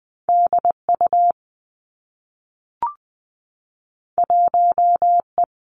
Morse code, heard as a keyed tone, about 19 wpm, 700 Hz. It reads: D U 1 E
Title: Two Challenging Letter-Number Combinations 20wpm